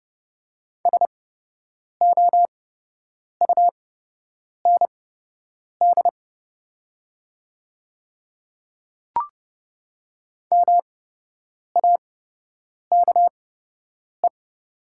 Morse code, heard as a keyed tone, about 30 wpm, spaced as for 8 wpm, 700 Hz. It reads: SOUND MAKE
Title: 8x Character Spacing - Top 100 Words - 30wpm